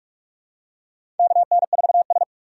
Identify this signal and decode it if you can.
Morse code, heard as a keyed tone, about 46 wpm, 700 Hz. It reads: KN4S
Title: USA Call Signs - 4 Characters - 6x Character Spacing - 45wpm